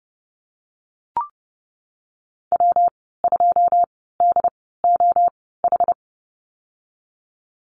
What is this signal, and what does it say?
Morse code, heard as a keyed tone, about 30 wpm, spaced as for 17 wpm, 700 Hz. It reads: W2DOH
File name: Lesson 19g - Letter w - Call Signs 30wpm